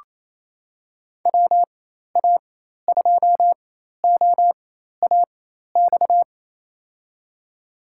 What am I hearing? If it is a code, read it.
Morse code, heard as a keyed tone, about 28 wpm, spaced as for 13 wpm, 700 Hz. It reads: WA2OAX
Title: USA Call Signs - 6 Characters - 4x Character Spacing - 28wpm